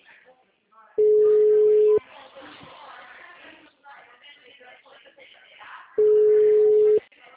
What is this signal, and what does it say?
Ringback tone (Europe, 425 Hz), with speech.
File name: out- -2010- - - .84216